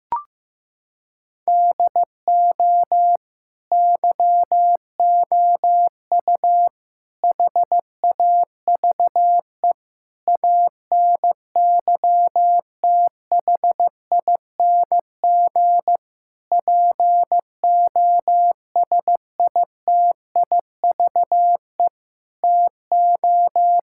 Morse code, heard as a keyed tone, about 16 wpm, 700 Hz. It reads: DO YOU HAVE ANYTHING POSITIVE TO
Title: Sentences from Top 1000 Words - Review Part 5 - 15wpm